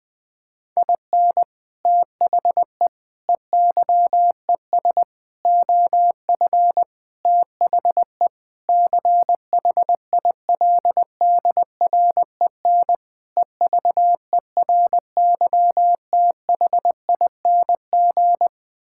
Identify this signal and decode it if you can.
Morse code, heard as a keyed tone, about 20 wpm, 700 Hz. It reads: IN THE EYES OF THE CHILDREN EVERYTHING
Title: Sentences with 5-Word Phrases - 20wpm